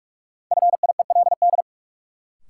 Morse code, heard as a keyed tone, about 46 wpm, 700 Hz. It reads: FIELD
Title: Sentences from Top 1000 Words - New Word Drill - 45wpm